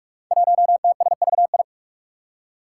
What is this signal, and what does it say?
Morse code, heard as a keyed tone, about 44 wpm, 700 Hz. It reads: 1TSVI
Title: Five Challenging Letter-Number Combinations 45wpm